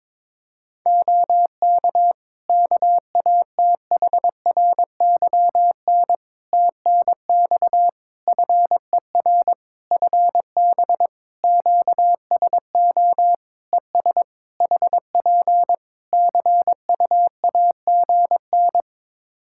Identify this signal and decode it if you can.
Morse code, heard as a keyed tone, about 22 wpm, 700 Hz. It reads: OK KATHRYN TNX FER FB QSO ES HP CUAGN
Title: K1USN SST QSO Practice 22wpm